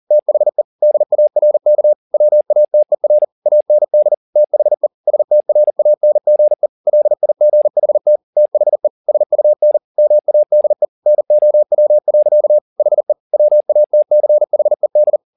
Morse code, heard as a keyed tone, about 40 wpm, 600 Hz. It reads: THE DARK WATER AND THE STRANGE LIGHT THE SUN MADE NOW. HE WATCHED